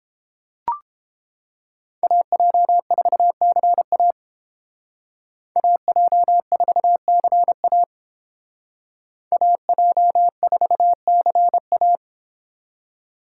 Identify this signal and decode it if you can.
Morse code, heard as a keyed tone, about 28 wpm, 700 Hz. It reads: AJ4CA AJ4CA AJ4CA